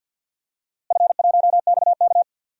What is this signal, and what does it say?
Morse code, heard as a keyed tone, about 52 wpm, 700 Hz. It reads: F1XK
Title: DXCC-100 Call Signs - 4 Characters - Repeated 2x - 50wpm